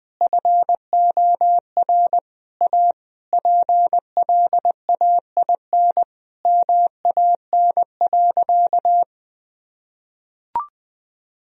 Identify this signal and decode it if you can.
Morse code, heard as a keyed tone, about 19 wpm, 700 Hz. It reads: FOR A PLAIN MAN.
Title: Sentences from Top 1000 Words - Review Part 2 - 20wpm